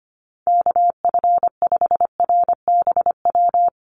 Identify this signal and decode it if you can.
Morse code, heard as a keyed tone, about 25 wpm, 700 Hz. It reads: KF5RBW
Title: USA Call Signs Repeated 3x 25wpm